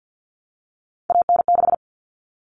Morse code, heard as a keyed tone, about 50 wpm, 700 Hz. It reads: AN6
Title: All Cut Numbers - 3-Digit 50wpm